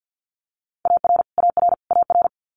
Morse code, heard as a keyed tone, about 50 wpm, 700 Hz. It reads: AR AR AR
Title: Multi-letter Phonemes Repeated 3x 50wpm